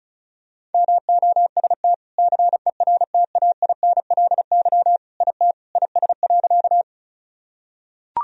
Morse code, heard as a keyed tone, about 34 wpm, 700 Hz. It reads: MOST CERTAINLY IT IS.
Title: Sentences from Top 1000 Words - Review Part 1 - 35wpm